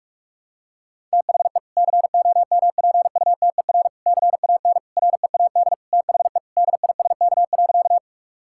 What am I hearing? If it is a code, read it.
Morse code, heard as a keyed tone, about 46 wpm, 700 Hz. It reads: THE COMPUTER CAN READ THE DISK.
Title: Sentences from Top 1000 Words - Review Part 6 - 45wpm